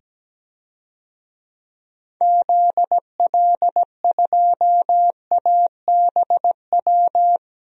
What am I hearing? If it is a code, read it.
Morse code, heard as a keyed tone, about 17 wpm, 700 Hz. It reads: ZL2ABW
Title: DXCC-100 Call Signs - 6 Characters - 17wpm